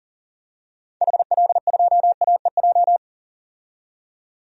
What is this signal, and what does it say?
Morse code, heard as a keyed tone, about 40 wpm, 700 Hz. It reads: HL2AEJ
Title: DXCC-100 Call Signs - 6 Characters - 40wpm